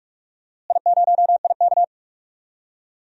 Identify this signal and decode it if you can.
Morse code, heard as a keyed tone, about 46 wpm, 690 Hz. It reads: I0IK
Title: DXCC-100 Call Signs - 4 Characters - Repeated 3x - 45wpm